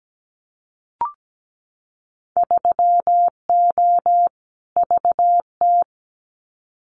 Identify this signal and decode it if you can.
Morse code, heard as a keyed tone, about 17 wpm, 700 Hz. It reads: 3O VT
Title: Winter Field Day Exchanges Encore 17wpm